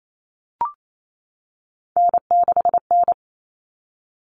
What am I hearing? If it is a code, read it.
Morse code, heard as a keyed tone, about 28 wpm, 700 Hz. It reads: N6N
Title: Common Cut Numbers - 3-Digit 28wpm